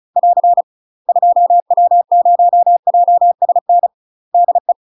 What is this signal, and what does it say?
Morse code, heard as a keyed tone, about 33 wpm, 700 Hz. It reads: <AR> 2W0JSN DE
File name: UK MASTER QSOs TRAINING / 35WPM / UK MASTER QSOs 0032